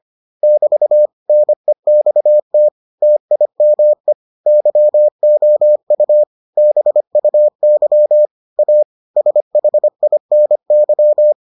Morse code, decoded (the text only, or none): <BT> NEXT TIME YOU BUY A SHINY